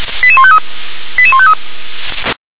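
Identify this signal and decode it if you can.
Five-tone selective call (ZVEI), heard as quick runs of five tones, tones of about 70 ms. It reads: 88153 88153
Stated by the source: ZVEI five-tone selective call received on an SDR.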